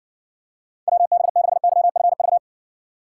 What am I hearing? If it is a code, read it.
Morse code, heard as a keyed tone, about 68 wpm, 700 Hz. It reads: WB6YFV